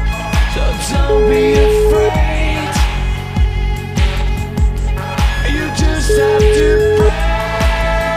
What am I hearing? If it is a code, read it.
Ringback tone (Europe, 425 Hz), with noise.